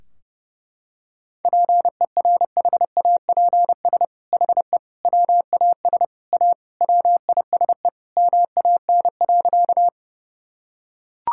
Morse code, heard as a keyed tone, about 30 wpm, 700 Hz. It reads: PERHAPS HE WAS A WISE MAN.
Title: Sentences from Top 1600 Words 30wpm